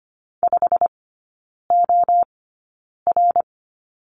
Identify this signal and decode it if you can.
Morse code, heard as a keyed tone, about 25 wpm, spaced as for 9 wpm, 700 Hz. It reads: 5OR
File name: course / 25:10wpm / 093 Lesson 16e - Letter c - Three Characters 25wpm